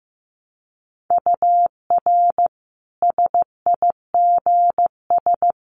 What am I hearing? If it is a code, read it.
Morse code, heard as a keyed tone, about 15 wpm, 700 Hz. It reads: UR SIGS
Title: Common QSO Phrases 15wpm